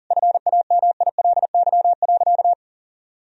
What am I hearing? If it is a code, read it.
Morse code, heard as a keyed tone, about 40 wpm, 700 Hz. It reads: FAMILY.
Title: Sentences from Top 1000 Words - Review Part 4 - 40wpm